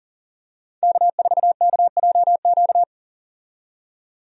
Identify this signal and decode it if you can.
Morse code, heard as a keyed tone, about 40 wpm, 700 Hz. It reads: K4KJQ
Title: USA Call Signs - 5 Characters - 40wpm